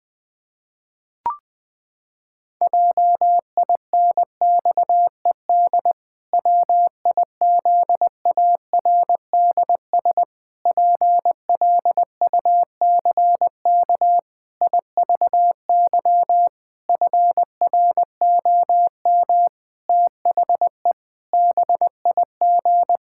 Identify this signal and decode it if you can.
Morse code, heard as a keyed tone, about 19 wpm, 700 Hz. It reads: JINXED WIZARDS PLUCK IVY FROM THE BIG
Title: Pangram Challenge v2 - 20wpm